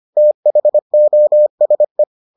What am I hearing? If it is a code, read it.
Morse code, heard as a keyed tone, about 25 wpm, 600 Hz. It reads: THOSE